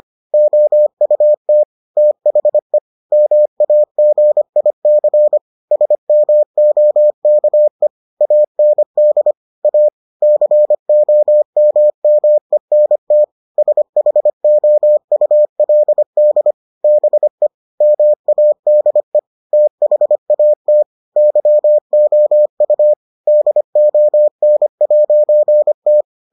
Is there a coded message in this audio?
Yes: Morse code from 0.3 to 26.0 s, heard as a keyed tone, about 25 wpm, 600 Hz. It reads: OUT THE MAGIC SMOKE AND A COMMENT SHOULD BE MADE THAT YOU DON'T